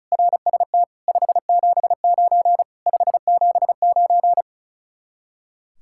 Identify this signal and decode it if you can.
Morse code, heard as a keyed tone, about 39 wpm, 700 Hz. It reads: RST 579 579
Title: Speed-Racing Common QSO Phrases - 35wpm